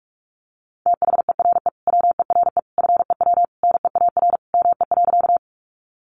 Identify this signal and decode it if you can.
Morse code, heard as a keyed tone, about 44 wpm, 700 Hz. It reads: THERE WERE FEW NEAR ME.
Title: Sentences from Top 300 Words 45wpm